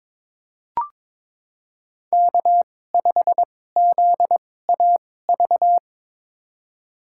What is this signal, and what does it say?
Morse code, heard as a keyed tone, about 21 wpm, spaced as for 16 wpm, 700 Hz. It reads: K5ZAV
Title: USA Call Signs - 5 Characters - 2x Character Spacing - 22wpm